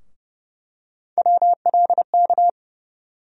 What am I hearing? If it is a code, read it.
Morse code, heard as a keyed tone, about 30 wpm, 700 Hz. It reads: WLK
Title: USA Call Sign Postfixes - 3 Letters 30wpm